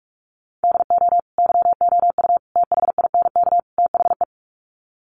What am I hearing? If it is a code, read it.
Morse code, heard as a keyed tone, about 44 wpm, 700 Hz. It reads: DO YOU THINK THE
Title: Sets of 4 Words 45wpm